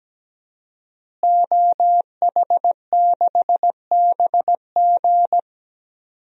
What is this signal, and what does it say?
Morse code, heard as a keyed tone, about 17 wpm, 700 Hz. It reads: OH6BG
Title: DXCC-100 Call Signs - 5 Characters - Repeated 2x - 17wpm